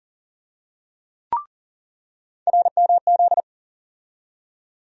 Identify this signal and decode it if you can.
Morse code, heard as a keyed tone, about 40 wpm, 700 Hz. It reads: RMZ
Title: USA Call Sign Postfixes - 3 Letters 40wpm